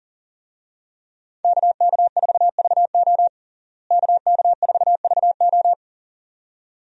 Morse code, heard as a keyed tone, about 37 wpm, 700 Hz. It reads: KK4VO KK4VO
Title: USA Call Signs - 5 Characters - Repeated 2x - 40wpm